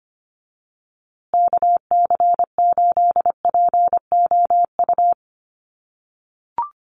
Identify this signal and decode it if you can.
Morse code, heard as a keyed tone, about 25 wpm, 700 Hz. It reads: KC8POU
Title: USA Call Signs Repeated 3x 25wpm